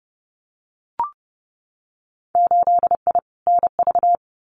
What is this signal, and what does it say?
Morse code, heard as a keyed tone, about 30 wpm, 700 Hz. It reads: 8I NV
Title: Winter Field Day Exchanges Encore 30wpm